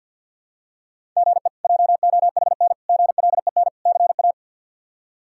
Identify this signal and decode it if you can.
Morse code, heard as a keyed tone, about 46 wpm, 700 Hz. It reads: GE JOHN GLEN CA
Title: K1USN SST QSO Practice - 50wpm